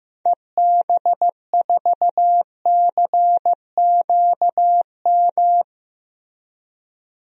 Morse code, heard as a keyed tone, about 15 wpm, 700 Hz. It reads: EB4CQM
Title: DXCC-100 Call Signs - 6 Characters - Repeated 3x - 15wpm